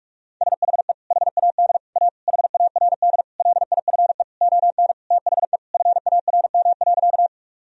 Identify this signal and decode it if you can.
Morse code, heard as a keyed tone, about 41 wpm, 700 Hz. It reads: SHE HAD A HARD LIFE ON THE FARM.
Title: Sentences from Top 1000 Words - Review Part 2 - 45wpm